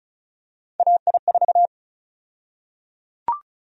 Morse code, heard as a keyed tone, about 36 wpm, 700 Hz. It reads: AI4
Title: USA Call Sign Prefixes - 35wpm